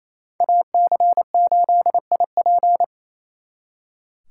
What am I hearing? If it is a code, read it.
Morse code, heard as a keyed tone, about 36 wpm, 700 Hz. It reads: AC8IP